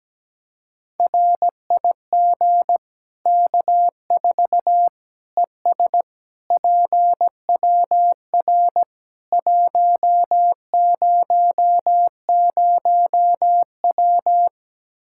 Morse code, heard as a keyed tone, about 17 wpm, 700 Hz. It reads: RIG K4 ES PWR 100W
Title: K1USN SST QSO Practice 17wpm